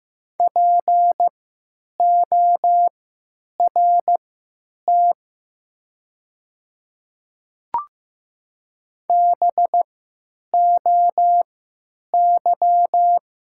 Morse code, heard as a keyed tone, about 15 wpm, spaced as for 9 wpm, 700 Hz. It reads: PORT BOY
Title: 3x Character Spacing - Top 300 Words - 15wpm